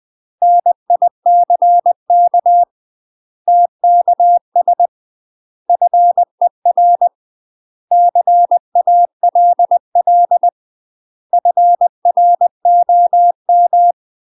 Morse code, heard as a keyed tone, about 20 wpm, 700 Hz. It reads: NICK TKS FER CALL FROM
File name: UK MASTER QSOs TRAINING / 20WPM / UK MASTER QSOs 0015